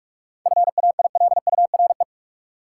Morse code, heard as a keyed tone, about 45 wpm, 700 Hz. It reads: FAILURE
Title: Sentences from Top 900 Words - New Word Drill - 45wpm